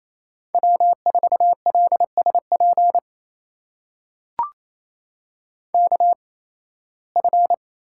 Morse code, heard as a keyed tone, about 28 wpm, spaced as for 18 wpm, 700 Hz. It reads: W4LSP KF
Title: USA Call Signs - 5 Characters - 8x Character Spacing - 28wpm